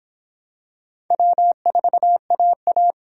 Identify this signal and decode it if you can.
Morse code, heard as a keyed tone, about 23 wpm, 700 Hz. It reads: W4AA